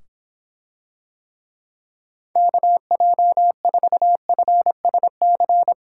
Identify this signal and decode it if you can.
Morse code, heard as a keyed tone, about 26 wpm, 700 Hz. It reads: KJ4FSC